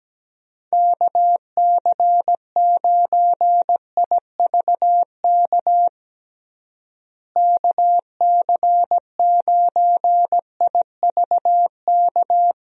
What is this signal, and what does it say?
Morse code, heard as a keyed tone, about 17 wpm, 700 Hz. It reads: KC9IVK KC9IVK
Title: USA Call Signs - 6 Characters - Repeated 3x - 17wpm